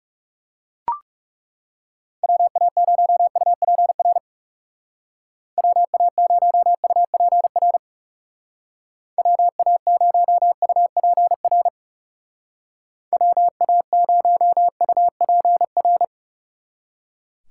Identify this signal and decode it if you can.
Morse code, heard as a keyed tone, about 37 wpm, 700 Hz. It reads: WA0UPR WA0UPR WA0UPR WA0UPR
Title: Speed Racing - USA Call Signs 30wpm